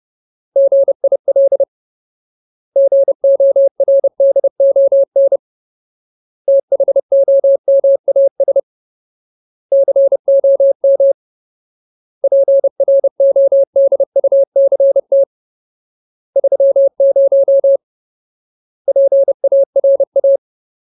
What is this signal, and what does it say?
Morse code, heard as a keyed tone, about 30 wpm, 550 Hz. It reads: GIL GORDON THOMAS COM PRODUCT 30 PARA
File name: Ultimate Word List / 30WPM Ultimate